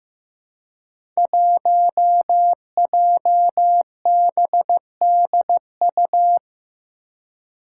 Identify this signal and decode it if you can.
Morse code, heard as a keyed tone, about 15 wpm, 700 Hz. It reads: 1JBDU
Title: Five Challenging Letter-Number Combinations 15wpm